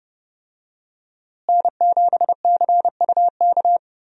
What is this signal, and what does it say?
Morse code, heard as a keyed tone, about 30 wpm, 700 Hz. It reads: N7CUK
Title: USA Call Signs - 5 Characters - Repeated 3x - 30wpm